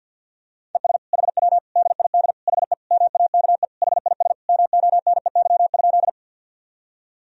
Morse code, heard as a keyed tone, about 51 wpm, 700 Hz. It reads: ES HW DID HE MAKE HIS MONEY?
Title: Sentences from Top 400 Words and Abbreviations 50wpm